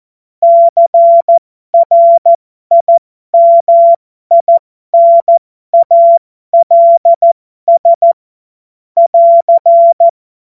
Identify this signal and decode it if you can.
Morse code, heard as a keyed tone, about 14 wpm, spaced as for 12 wpm, 680 Hz. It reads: CRIMINALS <AR>
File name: Morsekurs / Witze